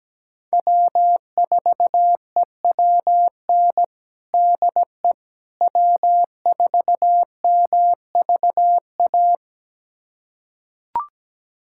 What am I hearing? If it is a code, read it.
Morse code, heard as a keyed tone, about 17 wpm, 700 Hz. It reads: W4EWN DE W4MVA
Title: USA Callsign DE Callsign - 5 Characters - 17wpm